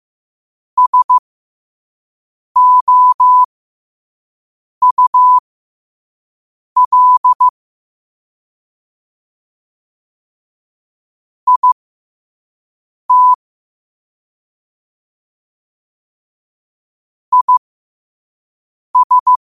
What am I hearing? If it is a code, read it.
Morse code, heard as a keyed tone, about 15 wpm, spaced as for 5 wpm, 1000 Hz. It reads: SOUL IT IS
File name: qotdfiles / 5wpm